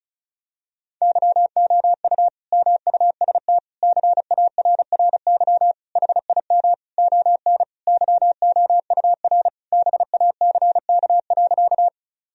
Morse code, heard as a keyed tone, about 34 wpm, 700 Hz. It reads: YOU MUST CARRY HIM ON YOUR BACK.
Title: Sentences from Top 1000 Words - Review Part 1 - 35wpm